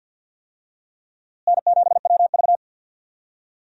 Morse code, heard as a keyed tone, about 52 wpm, 700 Hz. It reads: N7WV
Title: USA Call Signs - 4 Characters - Repeated 2x - 50wpm